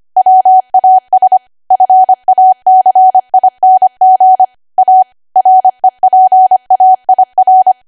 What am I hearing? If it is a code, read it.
Morse code, heard as a keyed tone, about 25 wpm, 750 Hz. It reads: WAS FACING A REPAIR
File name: Archive / 25 WPM / 25WPM